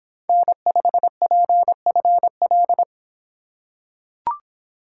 Morse code, heard as a keyed tone, about 25 wpm, 700 Hz. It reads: N5PFL